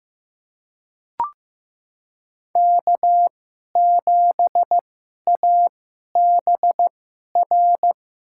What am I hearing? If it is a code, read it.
Morse code, heard as a keyed tone, about 15 wpm, spaced as for 11 wpm, 700 Hz. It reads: K7ABR
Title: USA Call Signs - 5 Characters - 2x Character Spacing - 15wpm